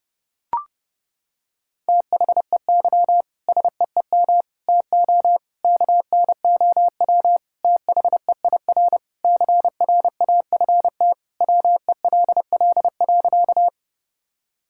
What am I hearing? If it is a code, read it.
Morse code, heard as a keyed tone, about 32 wpm, 700 Hz. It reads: THEY SEEM TO KNOW THEIR CRAFT WELL.
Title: Sentences from Top 1000 Words - Review Part 6 - 30wpm